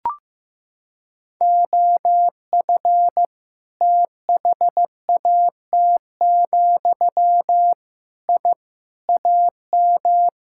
Morse code, heard as a keyed tone, about 15 wpm, 700 Hz. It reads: OF THAT, I AM
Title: Sentences from Top 1000 Words - Review Part 5 - 15wpm